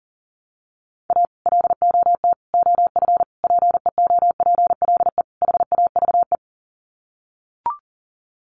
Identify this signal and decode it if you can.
Morse code, heard as a keyed tone, about 40 wpm, 700 Hz. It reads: A LOT OF PEOPLE HAVE
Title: Sets of 5 Words 40wpm